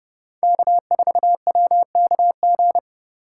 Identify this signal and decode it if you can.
Morse code, heard as a keyed tone, about 30 wpm, 700 Hz. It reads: K4WKG
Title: USA Call Signs - 5 Characters - 4x Character Spacing - 30wpm